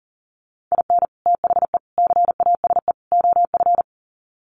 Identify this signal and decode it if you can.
Morse code, heard as a keyed tone, about 40 wpm, 700 Hz. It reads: IN THE CASE OF
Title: Sets of 4 Words 40wpm